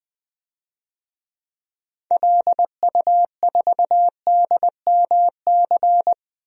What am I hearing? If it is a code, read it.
Morse code, heard as a keyed tone, about 21 wpm, 700 Hz. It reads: LU4DMC
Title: DXCC-100 Call Signs - 6 Characters - 20wpm